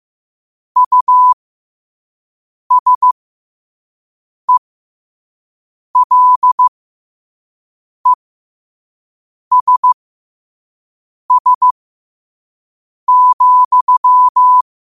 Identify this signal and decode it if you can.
Morse code, heard as a keyed tone, about 15 wpm, spaced as for 5 wpm, 1000 Hz. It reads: USELESS,
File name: qotdfiles / 5wpm